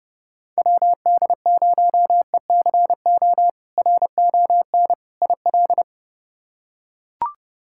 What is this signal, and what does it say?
Morse code, heard as a keyed tone, about 30 wpm, 700 Hz. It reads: WD0ECO RON IL
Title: K1USN SST QSO Practice - 30wpm